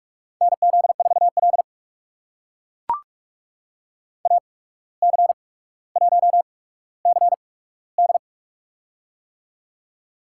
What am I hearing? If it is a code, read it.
Morse code, heard as a keyed tone, about 44 wpm, spaced as for 18 wpm, 700 Hz. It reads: NZ4L AC1CD
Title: USA Call Signs - 8x Character Spacing - 45wpm